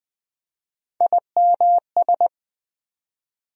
Morse code, heard as a keyed tone, about 20 wpm, 700 Hz. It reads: IMS
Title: USA Call Sign Postfixes - 3 Letters 20wpm